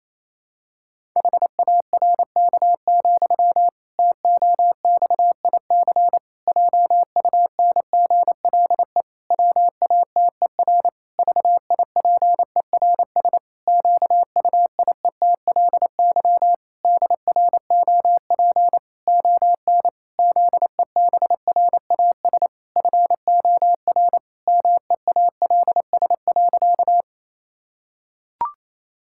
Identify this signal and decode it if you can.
Morse code, heard as a keyed tone, about 28 wpm, 700 Hz. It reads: HARK, TOXIC JUNGLE WATER VIPERS QUIETLY DROP ON ZEBRAS FOR MEALS.
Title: Pangram Challenge v2 - 28wpm